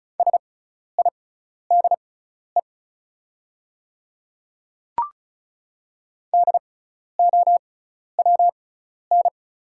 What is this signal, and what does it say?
Morse code, heard as a keyed tone, about 34 wpm, spaced as for 12 wpm, 700 Hz. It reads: SIDE DOWN
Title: 6x Character Spacing - Top 100 Words - 35wpm